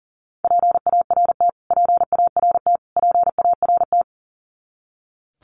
Morse code, heard as a keyed tone, about 40 wpm, 700 Hz. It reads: PART PART PART
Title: Top 300 Words Repeated 40wpm